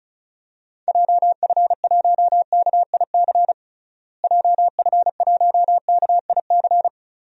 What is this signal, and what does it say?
Morse code, heard as a keyed tone, about 34 wpm, 700 Hz. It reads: JF1KIC JF1KIC
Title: DXCC-100 Call Signs - 6 Characters - Repeated 3x - 35wpm